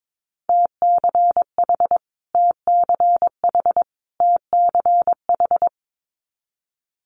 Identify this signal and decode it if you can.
Morse code, heard as a keyed tone, about 22 wpm, 700 Hz. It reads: TCH TCH TCH
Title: Multi-letter Phonemes Repeated 3x 22wpm